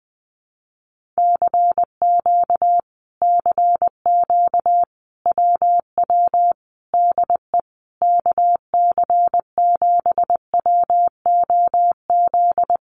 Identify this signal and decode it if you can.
Morse code, heard as a keyed tone, about 20 wpm, 700 Hz. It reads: CQ CQ WW DE KC7WOZ
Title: Common QSO Phrases Encore 20wpm